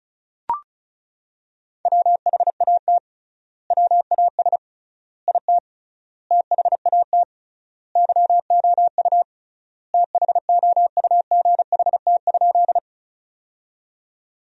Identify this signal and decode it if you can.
Morse code, heard as a keyed tone, about 36 wpm, 700 Hz. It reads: WHAT WAS IT THAT YOU THOUGHT?
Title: 3x Word-Spacing - Sentences from Top 300 Words - 35wpm